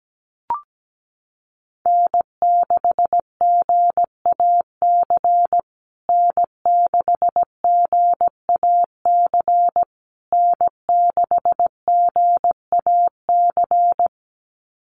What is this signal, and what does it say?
Morse code, heard as a keyed tone, about 17 wpm, 700 Hz. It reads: N6GAC N6GAC N6GAC
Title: USA Call Signs Repeated 3x 17wpm